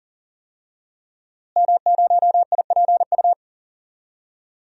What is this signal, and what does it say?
Morse code, heard as a keyed tone, about 40 wpm, 700 Hz. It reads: M0IPU
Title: DXCC-100 Call Signs - 5 Characters - Repeated 2x - 40wpm